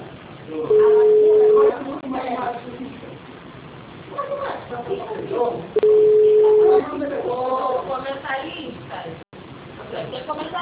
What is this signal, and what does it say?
Ringback tone (Europe, 425 Hz), with speech.